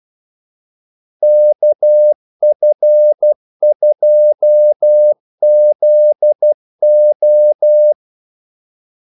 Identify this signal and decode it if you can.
Morse code, heard as a keyed tone, about 12 wpm, 600 Hz. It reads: KF2ZO
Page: 2 ekstra ordmellemrum mellem kaldesignalerne